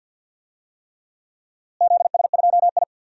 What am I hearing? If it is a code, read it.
Morse code, heard as a keyed tone, about 50 wpm, 700 Hz. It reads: ZS2I
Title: DXCC-100 Call Signs - 4 Characters - Repeated 3x - 50wpm